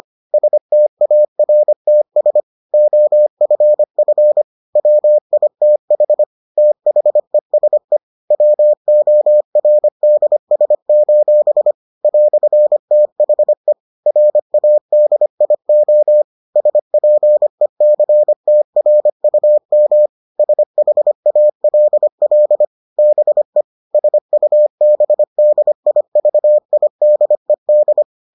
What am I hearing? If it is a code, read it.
Morse code, heard as a keyed tone, about 25 wpm, 600 Hz. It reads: STARTS OFF WITH THESE WORDS: "THE RADIO SPECTRUM SHALL BE SUBDIVIDED